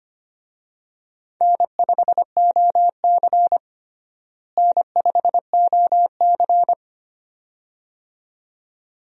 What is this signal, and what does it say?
Morse code, heard as a keyed tone, about 25 wpm, 700 Hz. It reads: N5OC N5OC
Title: USA Call Signs - 4 Characters - Repeated 2x - 25wpm